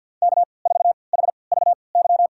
Morse code, heard as a keyed tone, about 52 wpm, spaced as for 29 wpm, 700 Hz. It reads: K4HVY
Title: USA Call Signs - 5 Characters - 3x Character Spacing - 50wpm